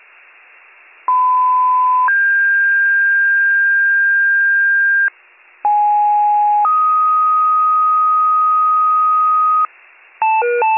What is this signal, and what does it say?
Two-tone page, heard as a one-second tone, then a three-second tone, about 1010 then 1600 Hz, about 830 then 1250 Hz, with noise. It is heard from 1.1 to 9.7 s.